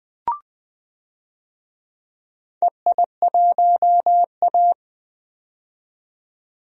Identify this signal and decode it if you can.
Morse code, heard as a keyed tone, about 21 wpm, 700 Hz. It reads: EI1A
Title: DXCC-100 Call Signs - 4 Characters - 20wpm